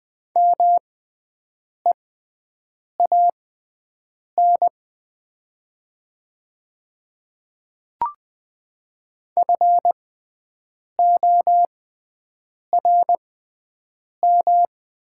Morse code, heard as a keyed tone, about 20 wpm, spaced as for 7 wpm, 700 Hz. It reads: MEAN FORM
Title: 6x Character Spacing - Top 300 Words - 20wpm